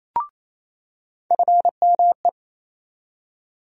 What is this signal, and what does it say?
Morse code, heard as a keyed tone, about 28 wpm, 700 Hz. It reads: FME
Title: USA Call Sign Postfixes - 3 Letters 28wpm